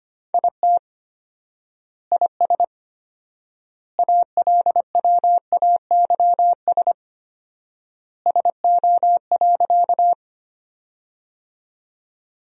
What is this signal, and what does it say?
Morse code, heard as a keyed tone, about 25 wpm, 700 Hz. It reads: IT IS ALWAYS SO.
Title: 4x Word-Spacing - Sentences from Top 300 Words - 25wpm